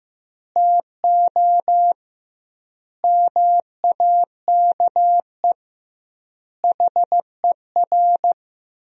Morse code, heard as a keyed tone, about 15 wpm, 700 Hz. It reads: TO MAKE HER
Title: 2x Word-Spacing - Sentences from Top 400 Words - 15wpm